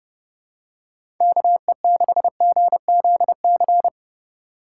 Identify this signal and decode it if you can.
Morse code, heard as a keyed tone, about 30 wpm, 700 Hz. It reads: KE6GZC
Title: USA Call Signs - 6 Characters - 3x Character Spacing - 30wpm